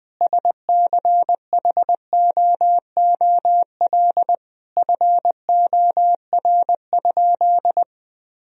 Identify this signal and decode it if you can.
Morse code, heard as a keyed tone, about 20 wpm, 700 Hz. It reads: SCHOOL FOR?
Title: Sentences from Top 1000 Words - Review Part 1 - 20wpm